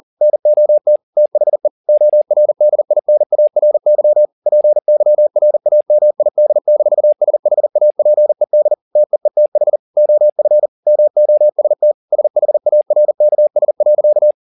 Morse code, heard as a keyed tone, about 40 wpm, 600 Hz. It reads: NOT THE ORDINARY PYRAMID-SHAPED TEETH OF MOST SHARKS.
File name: the old man and the sea 2 40wpm